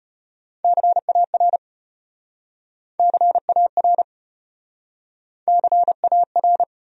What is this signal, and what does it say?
Morse code, heard as a keyed tone, about 33 wpm, 700 Hz. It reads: CAR CAR CAR